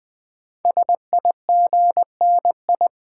Morse code, heard as a keyed tone, about 20 wpm, 700 Hz. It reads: SIGNI
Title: Build A Word - 5 Letters At A Time - 12-Letter Words - 20wpm